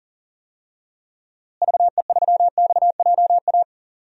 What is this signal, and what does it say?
Morse code, heard as a keyed tone, about 40 wpm, 700 Hz. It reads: VE3XJA
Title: DXCC-100 Call Signs - 6 Characters - Repeated 2x - 40wpm